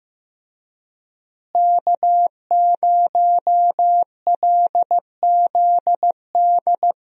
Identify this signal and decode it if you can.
Morse code, heard as a keyed tone, about 15 wpm, 700 Hz. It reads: K0LZD
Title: USA Call Signs - 5 Characters - Repeated 3x - 15wpm